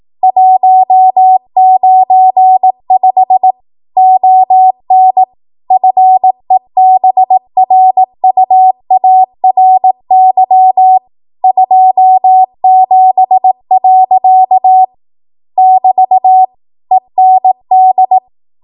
Morse code, heard as a keyed tone, about 18 wpm, 750 Hz. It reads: 195 ON FEBRUARY 27. <BT> END